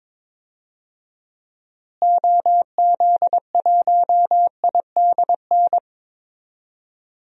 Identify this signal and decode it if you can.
Morse code, heard as a keyed tone, about 22 wpm, 700 Hz. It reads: OZ1IDN
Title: DXCC-100 Call Signs - 6 Characters - Repeated 3x - 22wpm